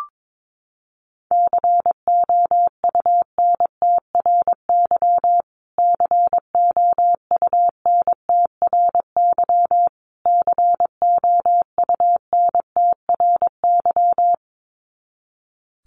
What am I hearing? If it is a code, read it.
Morse code, heard as a keyed tone, about 22 wpm, 700 Hz. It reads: COUNTRY COUNTRY COUNTRY
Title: Top 500 Words Repeated 22wpm